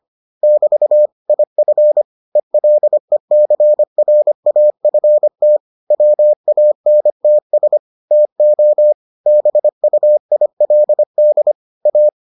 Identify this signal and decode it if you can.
Morse code, heard as a keyed tone, about 25 wpm, 600 Hz. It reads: <BT> IF ELECRAFT WANTS TO BUILD A